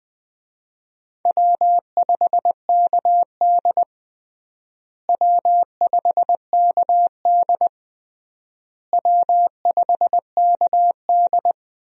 Morse code, heard as a keyed tone, about 20 wpm, 700 Hz. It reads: W5KD W5KD W5KD
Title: USA Call Signs - 4 Characters - Repeated 3x - 20wpm